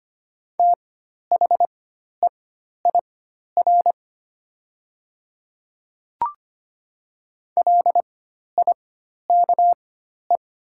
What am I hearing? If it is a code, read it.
Morse code, heard as a keyed tone, about 25 wpm, spaced as for 12 wpm, 690 Hz. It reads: THEIR LIKE
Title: 4x Character Spacing - Top 200 Words - 25wpm